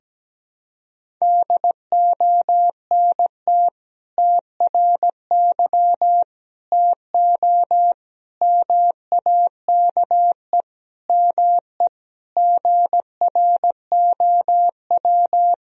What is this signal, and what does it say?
Morse code, heard as a keyed tone, about 17 wpm, 700 Hz. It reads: DONT TRY TO MAKE ME GROW
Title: Sentences from Top 1000 Words - Review Part 1 - 17wpm